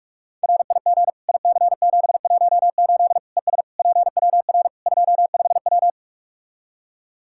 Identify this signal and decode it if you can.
Morse code, heard as a keyed tone, about 46 wpm, 700 Hz. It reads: RIG IC718 ES PWR 25W
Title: Practice QSO - Exchange 2 - Info - 45wpm